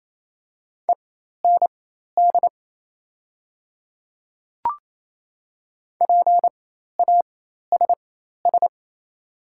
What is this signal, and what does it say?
Morse code, heard as a keyed tone, about 28 wpm, spaced as for 13 wpm, 700 Hz. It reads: END PASS
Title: 4x Character Spacing - Top 500 Words - 28wpm